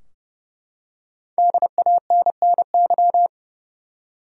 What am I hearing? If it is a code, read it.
Morse code, heard as a keyed tone, about 30 wpm, 700 Hz. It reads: DANNY
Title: USA Common Names - 30wpm